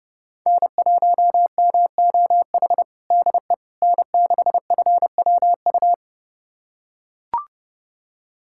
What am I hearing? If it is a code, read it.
Morse code, heard as a keyed tone, about 29 wpm, 700 Hz. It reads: N1MOH DE N6FWU
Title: USA Callsign DE Callsign - 5 Characters - 30wpm